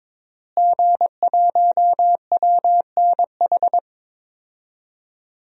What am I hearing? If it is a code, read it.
Morse code, heard as a keyed tone, about 22 wpm, 700 Hz. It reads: G1WNH
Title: DXCC-100 Call Signs - 5 Characters - Repeated 3x - 22wpm